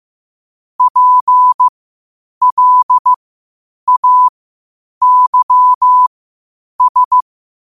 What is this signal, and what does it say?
Morse code, heard as a keyed tone, about 15 wpm, spaced as for 8 wpm, 1000 Hz. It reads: PLAYS
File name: qotdfiles / 8wpm